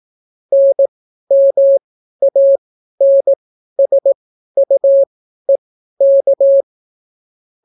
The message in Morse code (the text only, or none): NMANSUEK